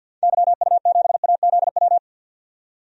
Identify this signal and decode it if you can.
Morse code, heard as a keyed tone, about 53 wpm, 700 Hz. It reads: YU7AZW
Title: DXCC-100 Call Signs - 6 Characters - 50wpm